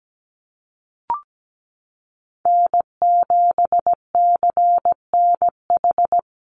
Morse code, heard as a keyed tone, about 17 wpm, 700 Hz. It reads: N7CNH
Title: NA SOTA Chaser Callsigns 17wpm